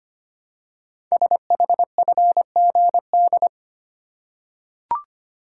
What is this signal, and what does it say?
Morse code, heard as a keyed tone, about 25 wpm, 700 Hz. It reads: SHFGD
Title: Code Groups - 5 Letters - 25wpm